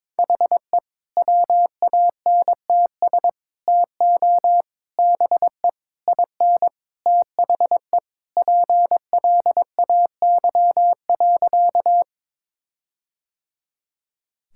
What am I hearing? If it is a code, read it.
Morse code, heard as a keyed tone, about 21 wpm, 700 Hz. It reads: HE WANTS TO BE IN THE PLAY.
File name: practice / 22wpm / Sentences with 4-Word Phrases - 22wpm